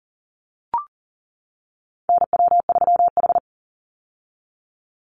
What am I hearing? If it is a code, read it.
Morse code, heard as a keyed tone, about 40 wpm, 700 Hz. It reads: NW3H
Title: USA Call Signs 40wpm